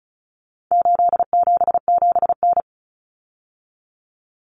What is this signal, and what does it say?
Morse code, heard as a keyed tone, about 34 wpm, 700 Hz. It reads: 877N
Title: Common Cut Numbers - 4-Digit 35wpm